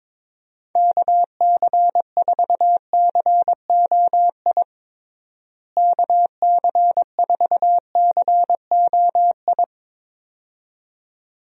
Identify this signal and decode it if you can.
Morse code, heard as a keyed tone, about 22 wpm, 700 Hz. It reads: KC4COI KC4COI
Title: USA Call Signs - 6 Characters - Repeated 2x - 22wpm